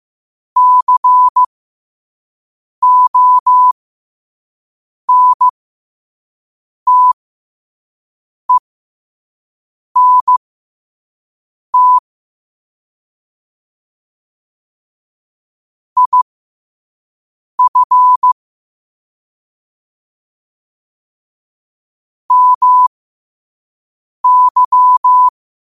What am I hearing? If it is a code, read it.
Morse code, heard as a keyed tone, about 15 wpm, spaced as for 5 wpm, 1000 Hz. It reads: CONTENT IF MY
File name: qotdfiles / 5wpm